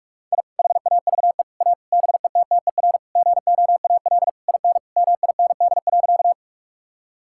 Morse code, heard as a keyed tone, about 42 wpm, 700 Hz. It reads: I HAVE A BETTER GOAL IN MIND.
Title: Sentences from Top 1000 Words - Review Part 5 - 45wpm